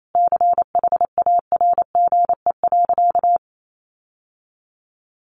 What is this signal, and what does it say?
Morse code, heard as a keyed tone, about 28 wpm, 700 Hz. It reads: CHARGE.
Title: Sentences from Top 900 Words 28wpm